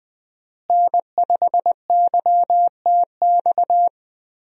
Morse code, heard as a keyed tone, about 20 wpm, 700 Hz. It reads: N5YTX
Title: USA Call Signs - 5 Characters - Repeated 3x - 20wpm